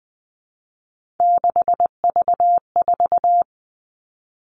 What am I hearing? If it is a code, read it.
Morse code, heard as a keyed tone, about 20 wpm, 700 Hz. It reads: 6V4
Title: Three Challenging Letter-Number Combinations 20wpm